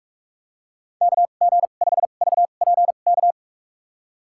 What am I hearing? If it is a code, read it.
Morse code, heard as a keyed tone, about 47 wpm, spaced as for 33 wpm, 700 Hz. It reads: KG5VPK
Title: USA Call Signs - 2x Character Spacing - 45wpm